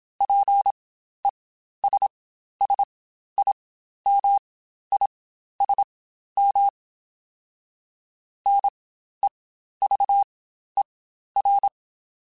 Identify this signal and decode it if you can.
Morse code, heard as a keyed tone, about 27 wpm, spaced as for 12 wpm, 800 Hz. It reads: PESSIMISM NEVER
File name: Quotes Spaced / Quotes Spaced-mp3s / Pessimism sp @25